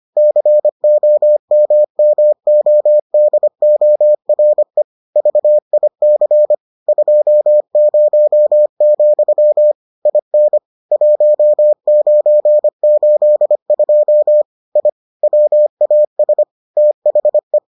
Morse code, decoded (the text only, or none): COMMODORE VIC 20, IN 1982 I WAS THE